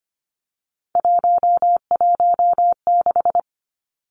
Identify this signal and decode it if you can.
Morse code, heard as a keyed tone, about 25 wpm, 700 Hz. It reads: 116
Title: Numbers - 3-Digit 25wpm